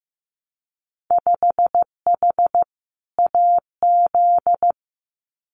Morse code, heard as a keyed tone, about 16 wpm, 700 Hz. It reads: 5H AZ
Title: Winter Field Day Exchanges Encore 15wpm